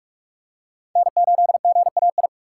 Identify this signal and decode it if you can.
Morse code, heard as a keyed tone, about 43 wpm, 700 Hz. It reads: N8GAI
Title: USA Call Signs - 5 Characters - 3x Character Spacing - 45wpm